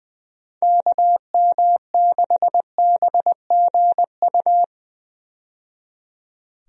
Morse code, heard as a keyed tone, about 20 wpm, 700 Hz. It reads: KM6BGU
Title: USA Call Signs - 6 Characters - 20wpm